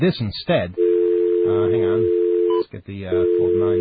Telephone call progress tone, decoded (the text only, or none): dial tone